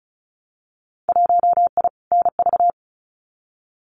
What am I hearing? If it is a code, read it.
Morse code, heard as a keyed tone, about 34 wpm, 700 Hz. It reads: 1I NV
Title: Winter Field Day Exchanges Encore 35wpm